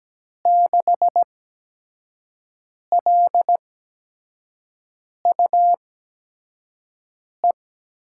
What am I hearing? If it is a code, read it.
Morse code, heard as a keyed tone, about 17 wpm, spaced as for 5 wpm, 700 Hz. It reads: 6LUE
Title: USA Call Signs - 6 Characters - 8x Character Spacing - 17wpm